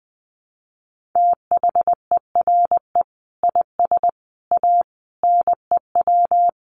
Morse code, heard as a keyed tone, about 19 wpm, 700 Hz. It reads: THERE IS A NEW
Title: Sentences from Top 1100 Words 20wpm